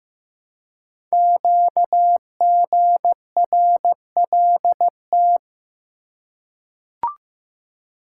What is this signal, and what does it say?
Morse code, heard as a keyed tone, about 15 wpm, 700 Hz. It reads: QGRLT
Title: Code Groups - 5 Letters - 15wpm